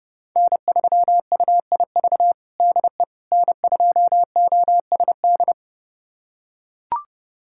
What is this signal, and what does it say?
Morse code, heard as a keyed tone, about 30 wpm, 700 Hz. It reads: N3UIV DE N2OSD
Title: USA Callsign DE Callsign - 5 Characters - 30wpm